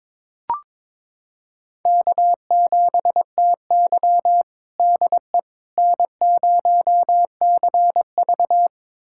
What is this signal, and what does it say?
Morse code, heard as a keyed tone, about 22 wpm, 700 Hz. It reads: K7TY DE N0CV
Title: USA Callsign DE Callsign - 4 Characters - 22wpm